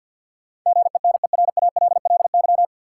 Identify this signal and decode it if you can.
Morse code, heard as a keyed tone, about 53 wpm, 700 Hz. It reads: GENERALLY
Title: Sentences from Top 600 Words - New Word Drill - 50wpm